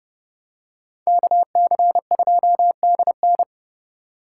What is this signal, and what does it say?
Morse code, heard as a keyed tone, about 29 wpm, 700 Hz. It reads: KC2DN